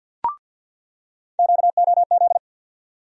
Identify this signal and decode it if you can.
Morse code, heard as a keyed tone, about 49 wpm, 700 Hz. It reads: QOZ
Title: USA Call Sign Postfixes - 3 Letters 50wpm